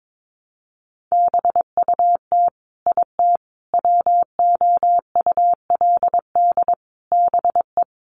Morse code, heard as a keyed tone, about 22 wpm, 700 Hz. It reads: BUT IT WOULD BE
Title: Sentences from Top 600 Words 22wpm